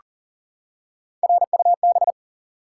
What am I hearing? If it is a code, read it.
Morse code, heard as a keyed tone, about 41 wpm, 700 Hz. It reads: RUB
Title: Three Letter Words - 40wpm